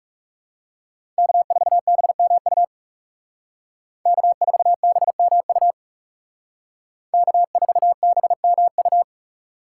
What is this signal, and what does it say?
Morse code, heard as a keyed tone, about 38 wpm, 700 Hz. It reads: K4BMU K4BMU K4BMU